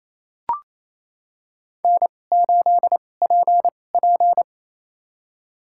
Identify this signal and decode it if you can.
Morse code, heard as a keyed tone, about 28 wpm, spaced as for 20 wpm, 700 Hz. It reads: N8PP
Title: USA Call Signs - 2x Character Spacing - 28wpm